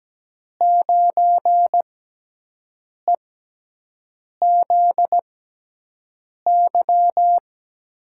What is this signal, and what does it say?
Morse code, heard as a keyed tone, about 17 wpm, spaced as for 6 wpm, 700 Hz. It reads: 9EZY
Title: USA Call Signs - 5 Characters - 6x Character Spacing - 17wpm